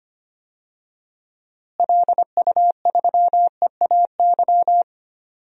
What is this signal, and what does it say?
Morse code, heard as a keyed tone, about 25 wpm, 700 Hz. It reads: LU3EAY
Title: DXCC-100 Call Signs - 6 Characters - Repeated 3x - 25wpm